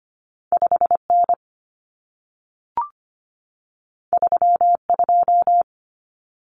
Morse code, heard as a keyed tone, about 25 wpm, 700 Hz. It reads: 5 N 3 2
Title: Common Cut Numbers - 2-Digit 25wpm